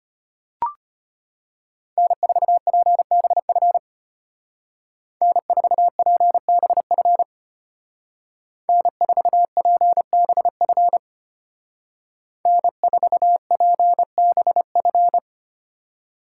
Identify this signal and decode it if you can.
Morse code, heard as a keyed tone, about 30 wpm, 700 Hz. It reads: N4PBF N4PBF N4PBF N4PBF
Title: Speed Racing - USA 1x3 Call Signs 25wpm